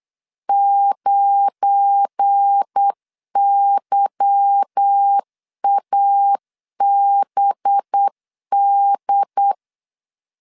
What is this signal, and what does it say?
Morse code, heard as a keyed tone, about 8 wpm, 800 Hz. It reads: 9YABD
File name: LEZIONI / 8 YJ45 / 8 Y 8wpm 3245